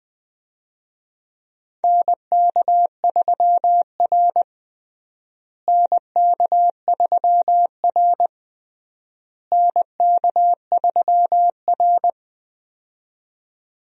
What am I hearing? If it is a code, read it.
Morse code, heard as a keyed tone, about 20 wpm, 700 Hz. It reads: NK3R NK3R NK3R
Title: USA Call Signs - 4 Characters - Repeated 3x - 20wpm